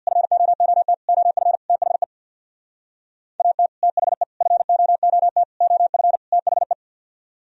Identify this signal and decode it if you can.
Morse code, heard as a keyed tone, about 64 wpm, 700 Hz. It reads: FOOT OF THE AT THE FOOT OF THE